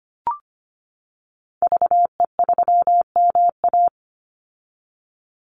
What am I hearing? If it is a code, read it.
Morse code, heard as a keyed tone, about 25 wpm, 700 Hz. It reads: VE3MA
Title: Call Signs - 25wpm